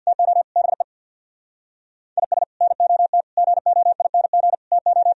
Morse code, heard as a keyed tone, about 64 wpm, 700 Hz. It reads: TO BE IS NOT GOING TO